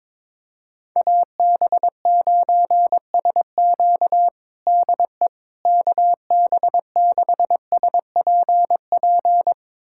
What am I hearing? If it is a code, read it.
Morse code, heard as a keyed tone, about 22 wpm, 700 Hz. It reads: AB9SQ DE KB6SPP
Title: USA Callsign DE Callsign 22wpm